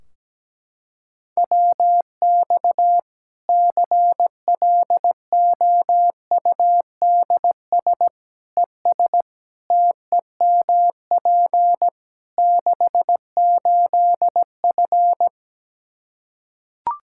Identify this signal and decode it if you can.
Morse code, heard as a keyed tone, about 17 wpm, 700 Hz. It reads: WX CLOUDS ES TEMP 68F
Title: Practice QSO - Exchange 2 - Info - 17wpm